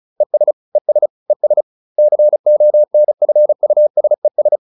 Morse code, heard as a keyed tone, about 34 wpm, 600 Hz. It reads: ES ES ES CONFUSES